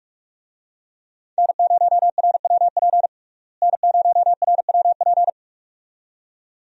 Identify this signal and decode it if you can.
Morse code, heard as a keyed tone, about 41 wpm, 700 Hz. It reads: N0RWP N0RWP
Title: USA Call Signs - 5 Characters - Repeated 2x - 45wpm